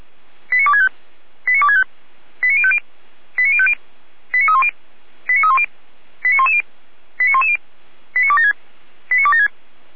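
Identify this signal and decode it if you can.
Five-tone selective call (ZVEI), heard as quick runs of five tones, tones of about 70 ms. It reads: 89265 89265 89050 89050 89310 89310 89110 89110 89276 89276
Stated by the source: ZVEI five-tone selective call received by SDR, fire brigade Feuerwehr Orlamünde.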